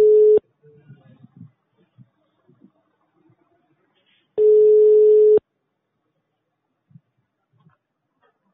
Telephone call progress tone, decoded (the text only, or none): ringback tone